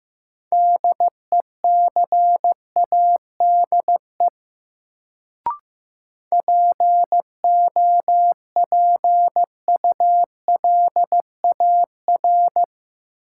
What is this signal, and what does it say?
Morse code, heard as a keyed tone, about 15 wpm, 700 Hz. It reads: DECADE POPULAR
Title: Sets of 5 Words - New Word Drill - 15wpm